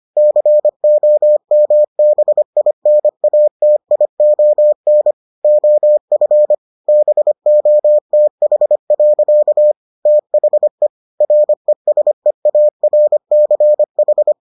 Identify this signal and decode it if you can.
Morse code, heard as a keyed tone, about 25 wpm, 600 Hz. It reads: COMBINATION OF BOTH. THE RESEARCH